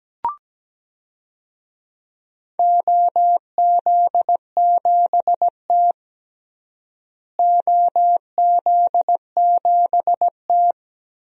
Morse code, heard as a keyed tone, about 17 wpm, 700 Hz. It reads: OZ7T OZ7T
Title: DXCC-100 Call Signs - 4 Characters - Repeated 3x - 17wpm